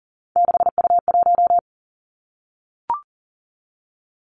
Morse code, heard as a keyed tone, about 40 wpm, 700 Hz. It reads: 6U1
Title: Three Challenging Letter-Number Combinations 40wpm